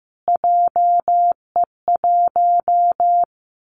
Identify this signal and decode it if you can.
Morse code, heard as a keyed tone, about 15 wpm, 700 Hz. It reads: JE1
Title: Callsign Prefixes for Top 25 DXCC - Repeated 3x - 15wpm